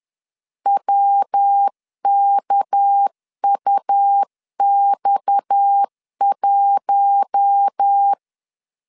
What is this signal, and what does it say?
Morse code, heard as a keyed tone, about 10 wpm, 800 Hz. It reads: WKUX1
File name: CW / DET / 1 6 1500C S C 10wpm 4252